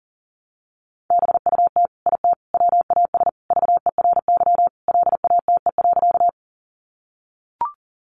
Morse code, heard as a keyed tone, about 40 wpm, 700 Hz. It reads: BUT IT WAS VERY LATE.
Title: Sentences from Top 300 Words 40wpm